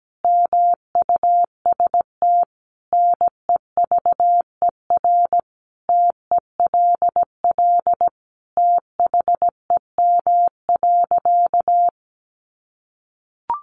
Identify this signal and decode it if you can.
Morse code, heard as a keyed tone, about 17 wpm, 700 Hz. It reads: MUST NEVER TELL THEM.
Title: Sentences from Top 900 Words 17wpm